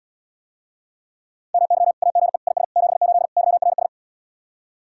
Morse code, heard as a keyed tone, about 80 wpm, 700 Hz. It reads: MY AGE IS 78 YRS